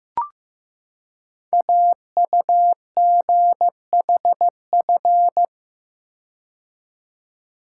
Morse code, heard as a keyed tone, about 15 wpm, 700 Hz. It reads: AUGHF
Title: Code Groups - 5 Letters - 15wpm